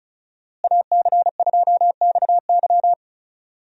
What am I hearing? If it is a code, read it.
Morse code, heard as a keyed tone, about 36 wpm, 700 Hz. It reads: AC2XY
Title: USA Call Signs - 5 Characters - Repeated 2x - 35wpm